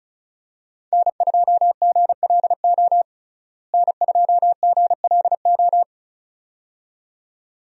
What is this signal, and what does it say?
Morse code, heard as a keyed tone, about 34 wpm, 700 Hz. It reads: N2GLO N2GLO
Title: USA Call Signs - 5 Characters - Repeated 2x - 35wpm